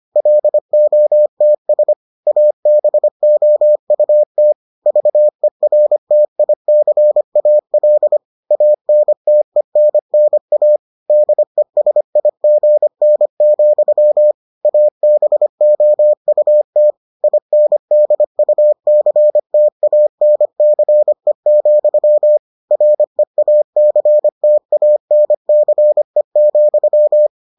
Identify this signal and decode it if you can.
Morse code, heard as a keyed tone, about 25 wpm, 600 Hz. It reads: LOTS ABOUT VERTICAL ANTENNA DESIGN, ABOUT INDUCTANCE, REACTANCE,